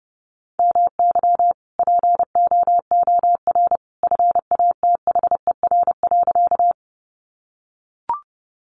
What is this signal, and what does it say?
Morse code, heard as a keyed tone, about 30 wpm, 700 Hz. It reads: MY POOR FATHER.
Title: Sentences from Top 1000 Words 30wpm